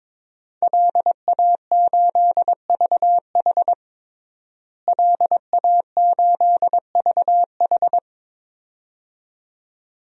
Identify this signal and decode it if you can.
Morse code, heard as a keyed tone, about 22 wpm, 700 Hz. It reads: LA8VH LA8VH
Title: DXCC-100 Call Signs - 5 Characters - Repeated 2x - 22wpm